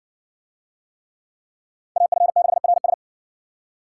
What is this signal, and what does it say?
Morse code, heard as a keyed tone, about 60 wpm, 700 Hz. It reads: AF6RS